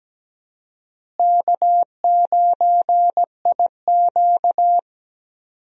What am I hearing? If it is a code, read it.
Morse code, heard as a keyed tone, about 17 wpm, 700 Hz. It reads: K9IQ
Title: USA Call Signs - 4 Characters - Repeated 2x - 17wpm